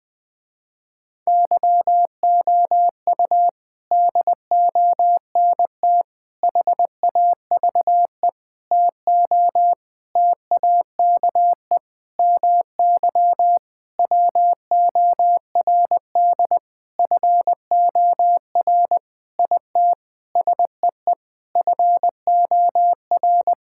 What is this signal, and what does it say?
Morse code, heard as a keyed tone, about 20 wpm, 700 Hz. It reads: YOU DONT HAVE TO TAKE MY WORD FOR IT SEE FOR
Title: Sentences with 8-Word Phrases - Spoken Beginning - 20wpm